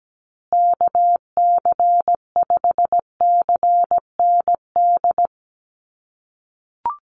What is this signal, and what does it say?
Morse code, heard as a keyed tone, about 17 wpm, 700 Hz. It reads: KC5CND
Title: USA Call Signs Repeated 2x 17wpm